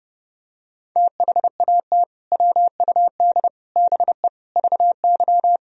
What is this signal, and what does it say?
Morse code, heard as a keyed tone, about 32 wpm, 700 Hz. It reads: THAT WUD BE VY
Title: Sentences from Top 200 Words and Abbreviations 30wpm